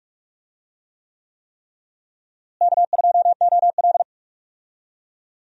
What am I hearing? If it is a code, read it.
Morse code, heard as a keyed tone, about 42 wpm, 700 Hz. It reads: K2OL
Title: USA Call Signs - 4 Characters - Repeated 3x - 45wpm